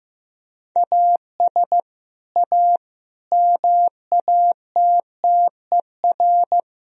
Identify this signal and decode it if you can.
Morse code, heard as a keyed tone, about 15 wpm, 700 Hz. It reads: AS A MATTER
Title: Sentences from Top 1000 Words - Review Part 3 - 15wpm